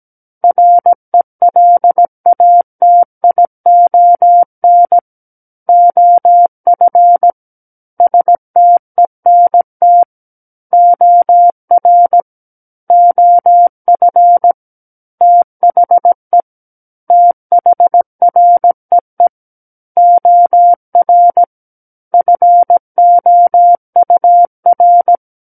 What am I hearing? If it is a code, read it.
Morse code, heard as a keyed tone, about 17 wpm, 700 Hz. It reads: RELATION OF STENT OR OF THE THREE OR FOUR
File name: Book One / 08-Chapter 8 - 17 WPM